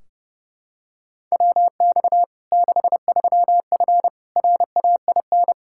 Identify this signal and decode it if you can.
Morse code, heard as a keyed tone, about 30 wpm, 700 Hz. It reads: WX 63F RAIN
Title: Practice QSO - Exchange 2 - Info - 30wpm